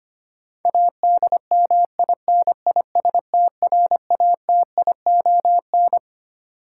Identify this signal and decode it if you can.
Morse code, heard as a keyed tone, about 25 wpm, 700 Hz. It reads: ADMINISTRATION
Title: Sets of 5 Words - New Word Drill - 25wpm